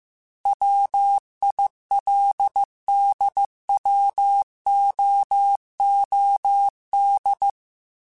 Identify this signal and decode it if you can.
Morse code, heard as a keyed tone, about 15 wpm, 800 Hz. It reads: WILDWOOD